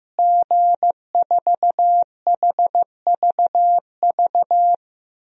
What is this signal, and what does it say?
Morse code, heard as a keyed tone, about 15 wpm, 700 Hz. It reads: G4HVV
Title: DXCC-100 Call Signs - 5 Characters - Repeated 3x - 15wpm